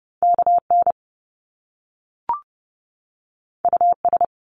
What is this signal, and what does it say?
Morse code, heard as a keyed tone, about 30 wpm, 700 Hz. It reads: KN US
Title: Top Two Letter Combinations in Words 30wpm